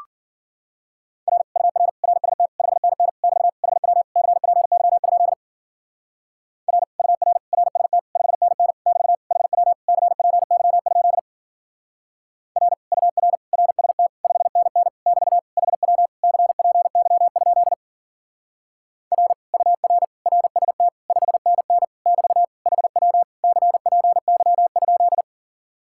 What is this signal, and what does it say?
Morse code, heard as a keyed tone, about 49 wpm, 700 Hz. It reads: R UR RST 5NN <BT> HW CPY? R UR RST 5NN <BT> HW CPY? R UR RST 5NN <BT> HW CPY? R UR RST 5NN <BT> HW CPY?
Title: Speed-Racing Common QSO Phrases - 40wpm